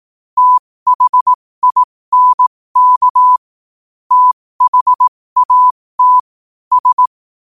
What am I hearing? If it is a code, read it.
Morse code, heard as a keyed tone, about 18 wpm, spaced as for 15 wpm, 1000 Hz. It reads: THINK THAT S
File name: qotdfiles / 15wpm